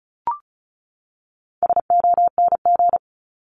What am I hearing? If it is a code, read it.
Morse code, heard as a keyed tone, about 34 wpm, 700 Hz. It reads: SONG
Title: Top 500 Words 35wpm